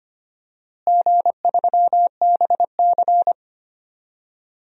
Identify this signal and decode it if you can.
Morse code, heard as a keyed tone, about 25 wpm, 700 Hz. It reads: G3BC
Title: DXCC-100 Call Signs - 4 Characters - Repeated 3x - 25wpm